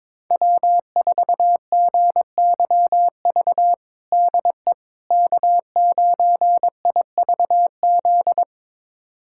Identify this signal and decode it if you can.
Morse code, heard as a keyed tone, about 22 wpm, 700 Hz. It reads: W4GYV DE K9IVZ
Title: USA Callsign DE Callsign - 5 Characters - 22wpm